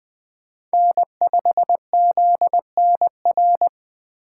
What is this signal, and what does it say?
Morse code, heard as a keyed tone, about 20 wpm, 700 Hz. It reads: N5ZNR
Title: USA Call Signs - 4x Character Spacing - 20wpm